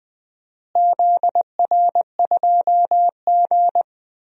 Morse code, heard as a keyed tone, about 20 wpm, 700 Hz. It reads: ZR2G
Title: DXCC-100 Call Signs - 4 Characters - Repeated 3x - 20wpm